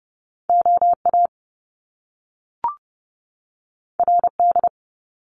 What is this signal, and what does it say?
Morse code, heard as a keyed tone, about 30 wpm, 700 Hz. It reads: OA RD
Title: Top Two Letter Combinations in Words 30wpm